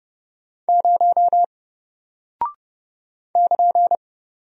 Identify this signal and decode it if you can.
Morse code, heard as a keyed tone, about 30 wpm, 700 Hz. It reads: 0 <KN>
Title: Single Character (Letter:Number:Punctuation:Prosign) Alt 30wpm